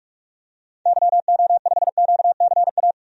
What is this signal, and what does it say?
Morse code, heard as a keyed tone, about 47 wpm, 700 Hz. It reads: YO5QCA
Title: DXCC-100 Call Signs - 6 Characters - 45wpm